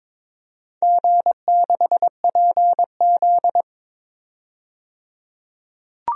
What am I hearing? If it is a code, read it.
Morse code, heard as a keyed tone, about 22 wpm, 700 Hz. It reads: G6PZ
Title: DXCC-100 Call Signs - 4 Characters - 22wpm